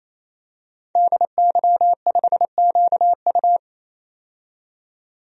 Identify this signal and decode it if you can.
Morse code, heard as a keyed tone, about 28 wpm, 700 Hz. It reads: DY5QU
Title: DXCC-100 Call Signs - 5 Characters - 28wpm